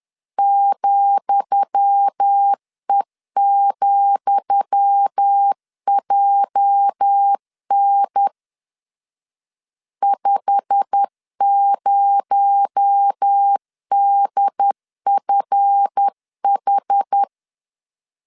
Morse code, heard as a keyed tone, about 10 wpm, 800 Hz. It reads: ,E,JN 50DFH
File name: LEZIONI / 10 SIGNS / 10D COMA 10wpm 3013